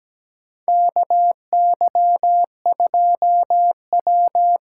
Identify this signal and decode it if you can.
Morse code, heard as a keyed tone, about 17 wpm, 700 Hz. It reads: KY2W
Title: USA Call Signs - 6x Character Spacing - 17wpm